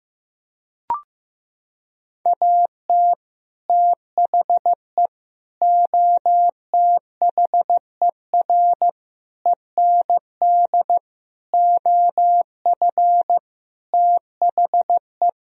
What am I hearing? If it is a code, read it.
Morse code, heard as a keyed tone, about 15 wpm, 700 Hz. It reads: AT THE OTHER END OF THE
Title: Sentences from Top 1000 Words - Review Part 2 - 15wpm